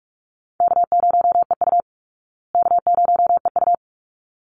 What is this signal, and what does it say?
Morse code, heard as a keyed tone, about 44 wpm, 700 Hz. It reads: K0EU K0EU
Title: Easy Call Signs Repeated 2x - 45wpm